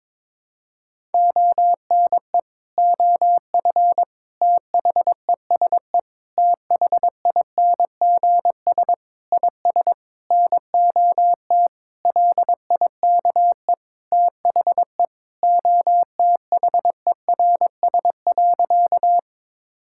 Morse code, heard as a keyed tone, about 22 wpm, 700 Hz. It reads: ONE OF THESE THINGS IS NOT LIKE THE OTHERS.
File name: practice / 22wpm / Sentences with 8-Word Phrases - 22wpm